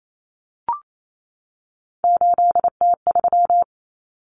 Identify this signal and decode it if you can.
Morse code, heard as a keyed tone, about 28 wpm, 700 Hz. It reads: 8T3
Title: Common Cut Numbers - 3-Digit 28wpm